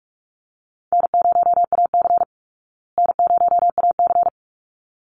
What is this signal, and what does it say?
Morse code, heard as a keyed tone, about 44 wpm, 700 Hz. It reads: N0AC N0AC
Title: Call Signs Repeated 3x - 45wpm